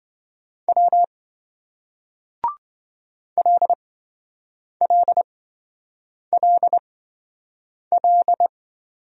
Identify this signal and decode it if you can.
Morse code, heard as a keyed tone, about 26 wpm, spaced as for 7 wpm, 700 Hz. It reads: W LLLL